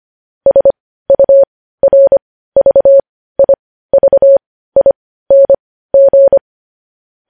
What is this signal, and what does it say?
Morse code, heard as a keyed tone, about 25 wpm, spaced as for 15 wpm, 550 Hz. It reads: SURVIVING
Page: news-headlines-in-morse-code-at-15-wpm